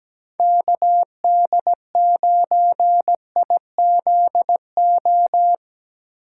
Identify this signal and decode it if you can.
Morse code, heard as a keyed tone, about 19 wpm, 700 Hz. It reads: KD9IZO
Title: Speed Racing - USA 2x3 Call Signs 17wpm